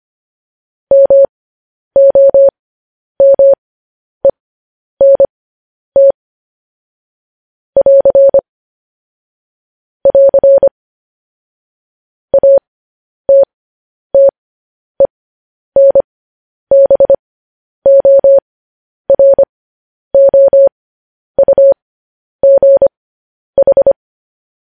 Morse code, heard as a keyed tone, about 25 wpm, spaced as for 10 wpm, 550 Hz. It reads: MOMENT <AR> <AR> ATTENBOROUGH
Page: News Headlines in Morse Code at 10 WPM